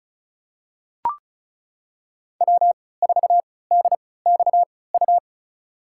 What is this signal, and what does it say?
Morse code, heard as a keyed tone, about 34 wpm, spaced as for 20 wpm, 690 Hz. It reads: W4DXU
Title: USA Call Signs - 5 Characters - 3x Character Spacing - 35wpm